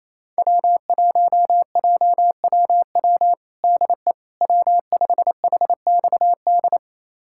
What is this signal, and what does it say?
Morse code, heard as a keyed tone, about 28 wpm, 700 Hz. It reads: W1JWW DE W5HXD
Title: USA Callsign DE Callsign 28wpm